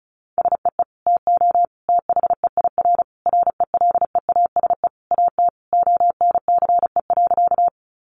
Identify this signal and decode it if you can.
Morse code, heard as a keyed tone, about 34 wpm, 700 Hz. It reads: SEE TO THEIR RELEASE AT ONCE.
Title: Sentences from Top 1100 Words 35wpm